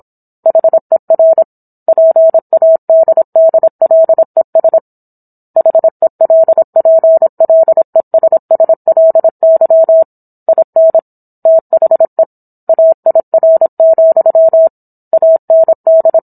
Morse code, heard as a keyed tone, about 26 wpm, 650 Hz. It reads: HER PADDLES HELPLESSLY IN THE AIR, AND